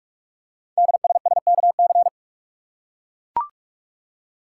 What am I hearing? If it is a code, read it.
Morse code, heard as a keyed tone, about 44 wpm, 700 Hz. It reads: DSSKC
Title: Code Groups - 5 Letters - 45wpm